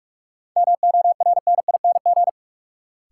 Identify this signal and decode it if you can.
Morse code, heard as a keyed tone, about 46 wpm, 700 Hz. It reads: MORNING
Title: Sentences from Top 400 Words - New Word Drill - 45wpm